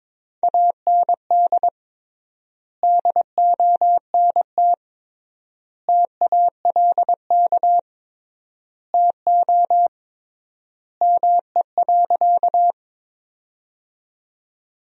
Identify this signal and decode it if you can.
Morse code, heard as a keyed tone, about 22 wpm, 700 Hz. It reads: AND DONT TALK TO ME.
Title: 3x Word-Spacing - Sentences from Top 400 Words - 22wpm